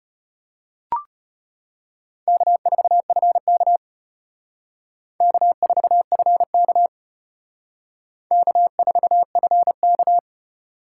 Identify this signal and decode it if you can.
Morse code, heard as a keyed tone, about 34 wpm, 700 Hz. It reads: K4FK K4FK K4FK